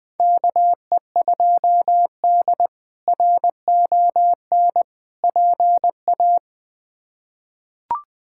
Morse code, heard as a keyed tone, about 21 wpm, 700 Hz. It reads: KE2D RON PA
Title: K1USN SST QSO Practice - 20wpm